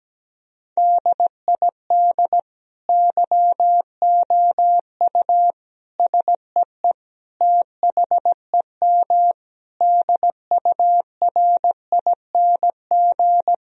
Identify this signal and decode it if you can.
Morse code, heard as a keyed tone, about 17 wpm, 700 Hz. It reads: DID YOU SEE THEM DURING
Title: Sentences from Top 1000 Words - Review Part 2 - 17wpm